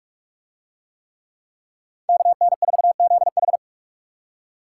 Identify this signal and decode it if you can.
Morse code, heard as a keyed tone, about 45 wpm, 700 Hz. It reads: KN4ZH